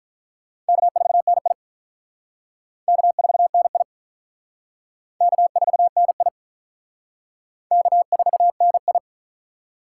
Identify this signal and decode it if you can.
Morse code, heard as a keyed tone, about 44 wpm, 700 Hz. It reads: K4NI K4NI K4NI K4NI